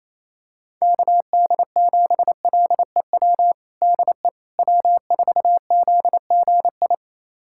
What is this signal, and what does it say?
Morse code, heard as a keyed tone, about 28 wpm, 700 Hz. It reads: KD7LEW DE W4ZGI
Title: USA Callsign DE Callsign 28wpm